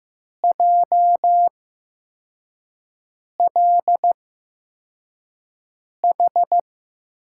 Morse code, heard as a keyed tone, about 15 wpm, spaced as for 4 wpm, 700 Hz. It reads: JLH
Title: USA Call Signs - 8x Character Spacing - 15wpm